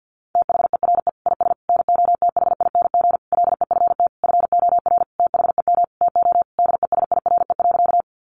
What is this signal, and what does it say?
Morse code, heard as a keyed tone, about 50 wpm, 700 Hz. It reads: THERE IS NOTHING LEFT FOR THEM TO DESIRE.
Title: Sentences from Top 1100 Words 50wpm